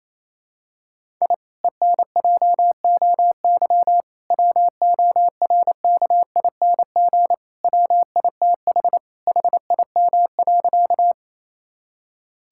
Morse code, heard as a keyed tone, about 28 wpm, 700 Hz. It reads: I ENJOY WORKING WITH HIM.
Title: Sentences from Top 1000 Words - Review Part 5 - 28wpm